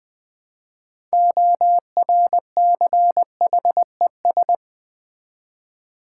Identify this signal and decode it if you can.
Morse code, heard as a keyed tone, about 20 wpm, 700 Hz. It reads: ORCHES
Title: Build A Word - 3 Letters At A Time - 9-Letter Words - 20wpm